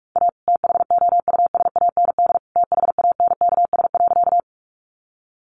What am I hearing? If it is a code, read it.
Morse code, heard as a keyed tone, about 44 wpm, 700 Hz. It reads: A THOUSAND THANKS.
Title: Sentences from Top 600 Words 45wpm